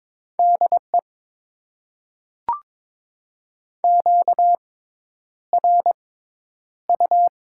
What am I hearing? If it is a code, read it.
Morse code, heard as a keyed tone, about 22 wpm, spaced as for 10 wpm, 700 Hz. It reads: DE QRU
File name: Top 90 QSO Elements - 6x Character Spacing - 22wpm